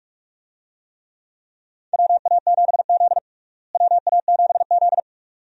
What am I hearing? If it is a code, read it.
Morse code, heard as a keyed tone, about 42 wpm, 700 Hz. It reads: WA7Z WA7Z
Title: USA Call Signs - 4 Characters - Repeated 2x - 45wpm